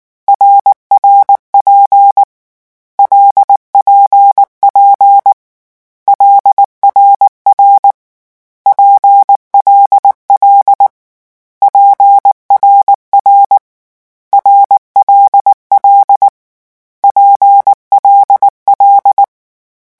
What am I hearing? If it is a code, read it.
Morse code, heard as a keyed tone, about 19 wpm, 800 Hz. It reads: RRP LPP LRR PLL PRR RLL PLL
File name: morse mp3 / lecon6-vitesse 18